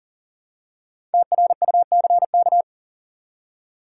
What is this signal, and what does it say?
Morse code, heard as a keyed tone, about 37 wpm, 700 Hz. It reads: TRUCK
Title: Five Letter Words - 40wpm